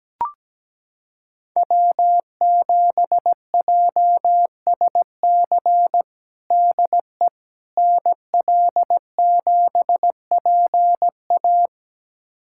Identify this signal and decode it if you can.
Morse code, heard as a keyed tone, about 17 wpm, 700 Hz. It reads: W7JSC DE NL7PA
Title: USA Callsign DE Callsign - 5 Characters - 17wpm